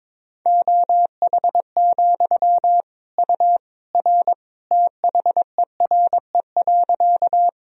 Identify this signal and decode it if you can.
Morse code, heard as a keyed tone, about 22 wpm, 700 Hz. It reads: OH, U R THERE.
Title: Sentences from Top 500 Words and Abbreviations 22wpm